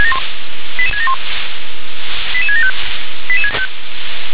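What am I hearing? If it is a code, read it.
Five-tone selective call (ZVEI), heard as quick runs of five tones, tones of about 70 ms. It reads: 88671 88675
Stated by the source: ZVEI five-tone call received by SDR.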